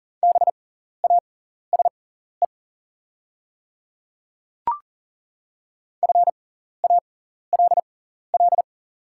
Morse code, heard as a keyed tone, about 40 wpm, spaced as for 14 wpm, 700 Hz. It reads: BASE FALL
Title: 6x Character Spacing - Top 500 Words - 40wpm